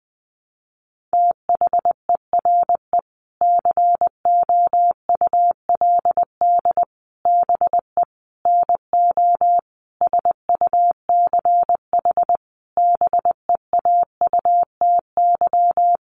Morse code, heard as a keyed tone, about 19 wpm, 700 Hz. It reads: THERE COULD BE NO SUCH BEAUTY
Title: Sentences from Top 600 Words 20wpm